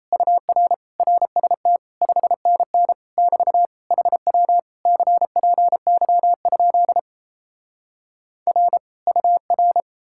Morse code, heard as a keyed tone, about 37 wpm, 700 Hz. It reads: UR RST 5NN <BT> HW CPY? R UR